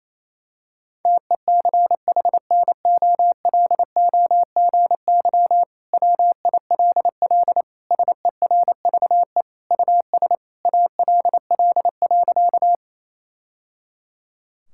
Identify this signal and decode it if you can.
Morse code, heard as a keyed tone, about 27 wpm, 700 Hz. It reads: TECHNOLOGY WILL SERVE US ALL.
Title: Sentences from Top 1000 Words - Review Part 5 - 28wpm